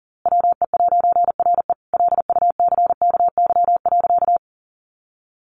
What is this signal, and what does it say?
Morse code, heard as a keyed tone, about 40 wpm, 700 Hz. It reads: WE'RE LUCKY.
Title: A Sound Of Thunder 40wpm